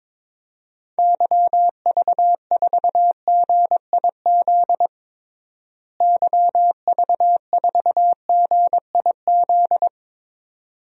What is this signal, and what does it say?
Morse code, heard as a keyed tone, about 21 wpm, 700 Hz. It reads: YV4GIZ YV4GIZ
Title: DXCC-100 Call Signs - 6 Characters - Repeated 2x - 22wpm